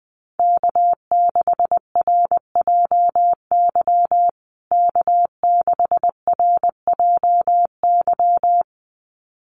Morse code, heard as a keyed tone, about 20 wpm, 700 Hz. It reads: K6RJY K6RJY
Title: USA Call Signs Repeated 2x 20wpm